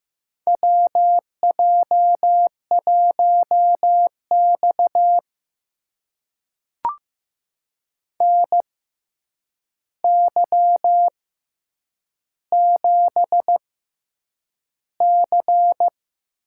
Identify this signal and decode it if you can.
Morse code, heard as a keyed tone, about 15 wpm, spaced as for 8 wpm, 700 Hz. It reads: WJ1X NY7C
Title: USA Call Signs - 4 Characters - 6x Character Spacing - 15wpm